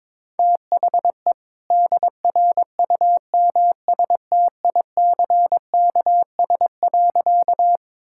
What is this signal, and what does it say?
Morse code, heard as a keyed tone, about 22 wpm, 690 Hz. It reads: THE DRUMSTICKS.
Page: morse-code-ninja-practice-22wpm